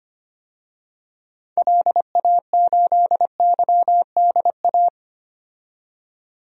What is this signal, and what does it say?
Morse code, heard as a keyed tone, about 25 wpm, 700 Hz. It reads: LA8YDA
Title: DXCC-100 Call Signs - 6 Characters - 25wpm